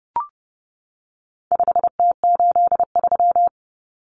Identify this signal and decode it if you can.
Morse code, heard as a keyed tone, about 30 wpm, 700 Hz. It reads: 5T83
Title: Common Cut Numbers - 4-Digit 30wpm